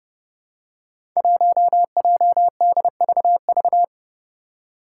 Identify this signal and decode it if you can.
Morse code, heard as a keyed tone, about 29 wpm, 700 Hz. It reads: 1JDVV
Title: Five Challenging Letter-Number Combinations 30wpm